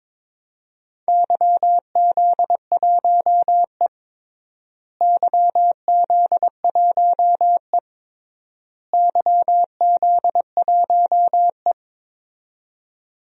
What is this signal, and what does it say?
Morse code, heard as a keyed tone, about 23 wpm, 700 Hz. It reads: YZ1E YZ1E YZ1E
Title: DXCC-100 Call Signs - 4 Characters - Repeated 3x - 22wpm